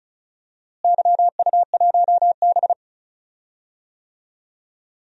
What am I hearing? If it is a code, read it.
Morse code, heard as a keyed tone, about 36 wpm, 700 Hz. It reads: YU1B
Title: DXCC-100 Call Signs - 4 Characters - Repeated 3x - 35wpm